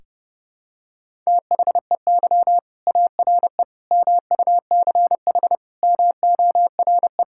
Morse code, heard as a keyed tone, about 32 wpm, 700 Hz. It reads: THEY ARE MUCH MORE
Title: Sentences from Top 1000 Words - Review Part 5 - 30wpm